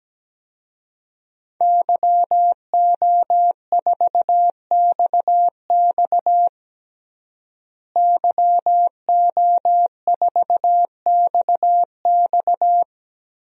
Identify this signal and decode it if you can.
Morse code, heard as a keyed tone, about 17 wpm, 700 Hz. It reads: YO4XX YO4XX
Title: DXCC-100 Call Signs - 5 Characters - Repeated 3x - 17wpm